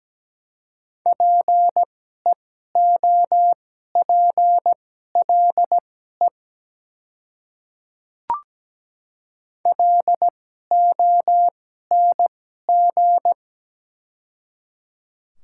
Morse code, heard as a keyed tone, about 17 wpm, spaced as for 12 wpm, 690 Hz. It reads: PEOPLE LONG
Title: 2x Character Spacing - Top 100 Words - 17wpm